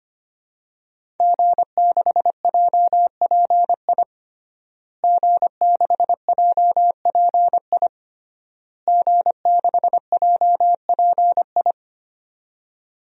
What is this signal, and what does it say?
Morse code, heard as a keyed tone, about 25 wpm, 700 Hz. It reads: G6JPI G6JPI G6JPI
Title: DXCC-100 Call Signs - 5 Characters - Repeated 3x - 25wpm